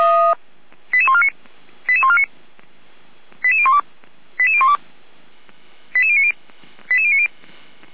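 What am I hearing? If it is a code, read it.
Five-tone selective call (ZVEI), heard as quick runs of five tones, tones of about 70 ms. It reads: 88149 88149 88012 88012 88090 88090
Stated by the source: ZVEI five-tone selective call received with an SDR on the Feuerwehr Orlamünde fire brigade server.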